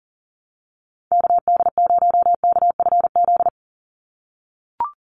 Morse code, heard as a keyed tone, about 40 wpm, 700 Hz. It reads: KD0KFZ
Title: USA Call Signs 40wpm